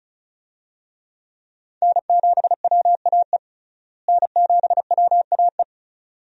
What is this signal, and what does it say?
Morse code, heard as a keyed tone, about 36 wpm, 700 Hz. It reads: N7WAE N7WAE
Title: USA Call Signs - 5 Characters - Repeated 2x - 35wpm